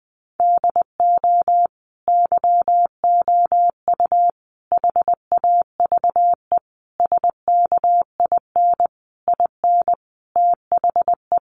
Morse code, heard as a keyed tone, about 20 wpm, 700 Hz. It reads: DO YOU HAVE SKIN IN THE
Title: Sentences from Top 1300 Words 20wpm